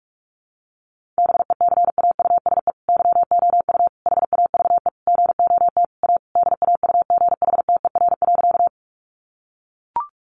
Morse code, heard as a keyed tone, about 44 wpm, 700 Hz. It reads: BECAUSE YOU HAVE GOT A DAUGHTER.
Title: Sentences from Top 1500 Words 45wpm